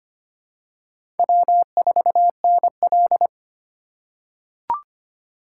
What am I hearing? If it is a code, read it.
Morse code, heard as a keyed tone, about 25 wpm, 700 Hz. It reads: W4NL
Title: USA Call Signs - 4x Character Spacing - 25wpm